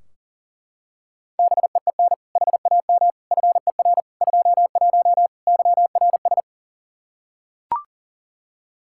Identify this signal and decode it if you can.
Morse code, heard as a keyed tone, about 37 wpm, 700 Hz. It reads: BEEN HAM FER 21 YRS
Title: Practice QSO - Exchange 3 - More Info - 40wpm